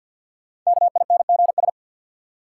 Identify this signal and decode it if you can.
Morse code, heard as a keyed tone, about 50 wpm, 700 Hz. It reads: KINGS
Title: Sentences from Top 1000 Words - Review Part 6 - New Word Drill - 50wpm